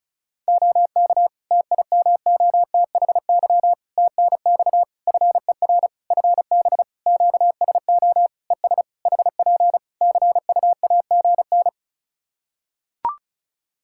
Morse code, heard as a keyed tone, about 36 wpm, 700 Hz. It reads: OK TIMOTHY TNX FER FB QSO ES HP CUAGN
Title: K1USN SST QSO Practice 35wpm